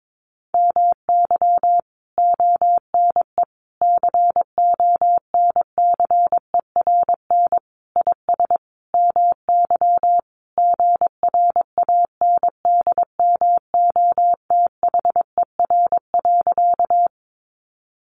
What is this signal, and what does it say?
Morse code, heard as a keyed tone, about 22 wpm, 700 Hz. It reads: MY ONE CONCERN IS MY GRANDMOTHER.
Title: Sentences from Top 1200 Words 22wpm